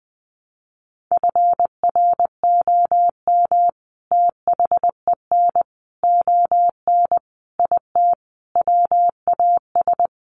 Morse code, heard as a keyed tone, about 20 wpm, 700 Hz. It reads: FROM THEN ON IT WAS
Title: Sentences from Top 400 Words 20wpm